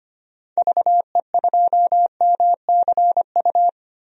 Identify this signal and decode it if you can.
Morse code, heard as a keyed tone, about 25 wpm, 700 Hz. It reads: VE2MCU
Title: DXCC-100 Call Signs - 6 Characters - Repeated 3x - 25wpm